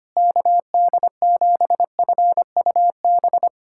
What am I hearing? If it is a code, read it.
Morse code, heard as a keyed tone, about 25 wpm, 700 Hz. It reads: KD7FUB
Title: USA Callsign DE Callsign - 6 Characters - 25wpm